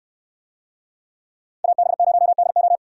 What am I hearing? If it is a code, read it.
Morse code, heard as a keyed tone, about 68 wpm, 700 Hz. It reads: AB0DO